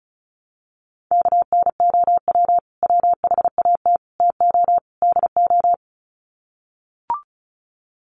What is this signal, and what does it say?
Morse code, heard as a keyed tone, about 34 wpm, 700 Hz. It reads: KNOW WHAT TO DO
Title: Sets of 4 Words 35wpm